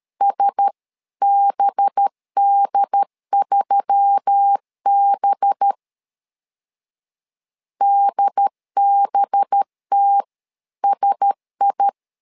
Morse code, heard as a keyed tone, about 13 wpm, 800 Hz. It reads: SBD3B DBTSI
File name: LEZIONI / 5 LBVC / 5 LB 12wpm 1906